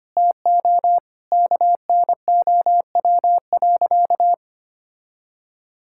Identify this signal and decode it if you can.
Morse code, heard as a keyed tone, about 25 wpm, 700 Hz. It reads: TO KNOW.
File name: practice / 25wpm / Sentences with 2-Word Phrases - 25wpm